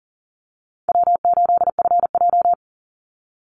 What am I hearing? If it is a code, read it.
Morse code, heard as a keyed tone, about 40 wpm, 700 Hz. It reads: W8FJ
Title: Easy Call Signs - 40wpm